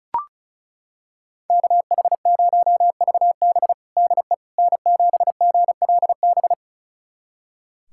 Morse code, heard as a keyed tone, about 36 wpm, 700 Hz. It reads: KH0VB DE N7GLB
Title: USA Callsign DE Callsign - 5 Characters - 35wpm